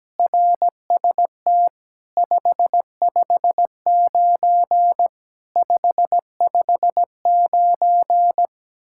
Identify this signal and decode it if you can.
Morse code, heard as a keyed tone, about 17 wpm, 700 Hz. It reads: RST 559 559
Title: Speed-Racing Common QSO Phrases Encore - 17wpm